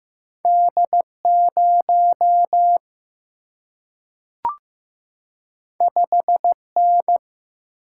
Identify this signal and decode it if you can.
Morse code, heard as a keyed tone, about 15 wpm, 700 Hz. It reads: D0 5N
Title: Single-Letter Single-Number 15wpm